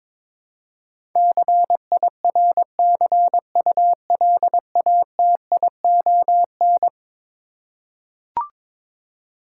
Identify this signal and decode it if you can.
Morse code, heard as a keyed tone, about 22 wpm, 700 Hz. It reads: CIRCULATION
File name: practice / 22wpm / Eleven Letter Words - 22wpm